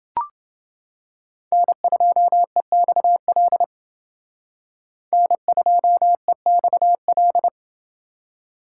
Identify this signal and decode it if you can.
Morse code, heard as a keyed tone, about 30 wpm, 700 Hz. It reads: N2EXL N2EXL